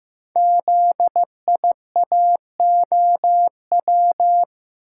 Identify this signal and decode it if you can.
Morse code, heard as a keyed tone, about 15 wpm, 700 Hz. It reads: ZIAOW
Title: Code Groups - 5 Letter:Numbers - 15wpm